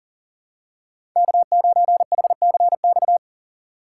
Morse code, heard as a keyed tone, about 40 wpm, 700 Hz. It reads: K9HCX
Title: USA Call Signs - 5 Characters - 6x Character Spacing - 40wpm